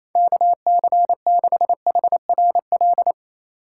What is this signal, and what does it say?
Morse code, heard as a keyed tone, about 28 wpm, 700 Hz. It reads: KC6HRL
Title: USA Call Signs - 3x Character Spacing - 28wpm